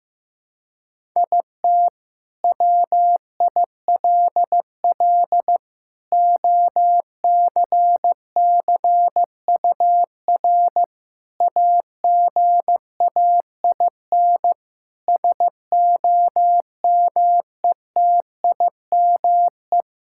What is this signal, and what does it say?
Morse code, heard as a keyed tone, about 15 wpm, 700 Hz. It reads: IT WILL OCCUR AGAIN SOMETIME
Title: Sentences from Top 1000 Words - Review Part 6 - 15wpm